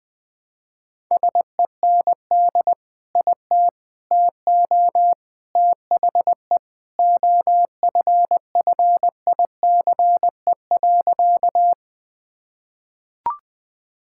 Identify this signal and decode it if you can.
Morse code, heard as a keyed tone, about 20 wpm, 700 Hz. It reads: SEND IT TO THE OFFICE.
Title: Sentences with 3-Word Phrases - 20wpm